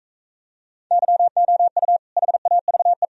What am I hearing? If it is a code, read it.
Morse code, heard as a keyed tone, about 38 wpm, 700 Hz. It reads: YOU HAVE